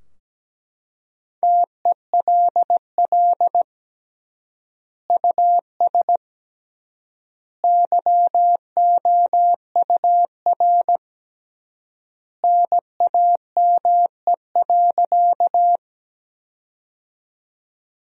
Morse code, heard as a keyed tone, about 17 wpm, 700 Hz. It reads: TELL US YOUR NAME.
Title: 3x Word-Spacing - Sentences from Top 200 Words - 17wpm